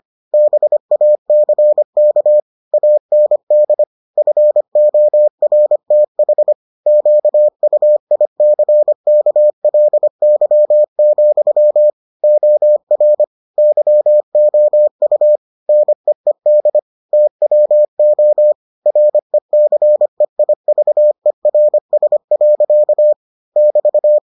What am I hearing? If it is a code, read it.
Morse code, heard as a keyed tone, about 25 wpm, 600 Hz. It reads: BACK AND FORTH QUICKLY, OR YOU NEED TWO RECEIVERS. <BT>